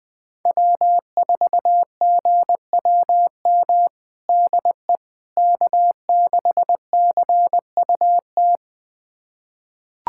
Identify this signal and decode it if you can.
Morse code, heard as a keyed tone, about 20 wpm, 700 Hz. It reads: W4GWM DE K6CUT
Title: USA Callsign DE Callsign - 5 Characters - 20wpm